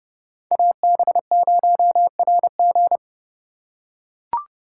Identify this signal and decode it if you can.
Morse code, heard as a keyed tone, about 32 wpm, 700 Hz. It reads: AB0RG
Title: USA Call Signs - 5 Characters - 3x Character Spacing - 30wpm